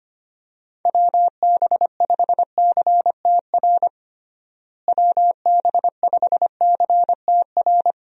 Morse code, heard as a keyed tone, about 25 wpm, 700 Hz. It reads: WB5CTR WB5CTR
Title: USA Call Signs - 6 Characters - Repeated 2x - 25wpm